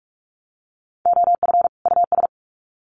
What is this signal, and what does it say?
Morse code, heard as a keyed tone, about 44 wpm, 700 Hz. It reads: OF US
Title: Sets of 2 Words 45wpm